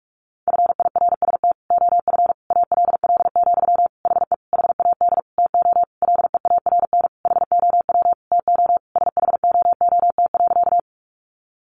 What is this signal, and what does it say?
Morse code, heard as a keyed tone, about 44 wpm, 700 Hz. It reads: FIRST OF ALL, HE HAD TO LEARN HOW TO SHOOT.
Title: Sentences from Top 1200 Words 45wpm